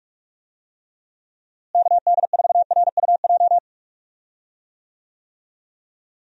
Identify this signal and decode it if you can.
Morse code, heard as a keyed tone, about 44 wpm, 700 Hz. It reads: KD4RUJ
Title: USA Call Signs - 6 Characters - Repeated 2x - 45wpm